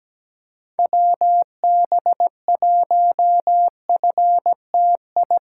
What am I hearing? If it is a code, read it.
Morse code, heard as a keyed tone, about 17 wpm, 700 Hz. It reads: WB1FTI
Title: USA Call Signs - 6 Characters - 6x Character Spacing - 17wpm